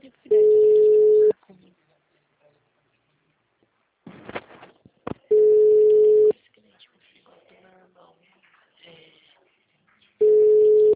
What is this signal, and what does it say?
Ringback tone (Europe, 425 Hz), with speech.
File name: out- -2016- - - .56324